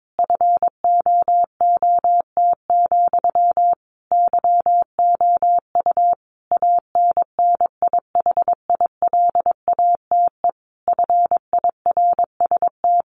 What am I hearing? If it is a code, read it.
Morse code, heard as a keyed tone, about 22 wpm, 700 Hz. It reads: FOOT, YOU ANNIHILATE FIRST
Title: A Sound Of Thunder 22wpm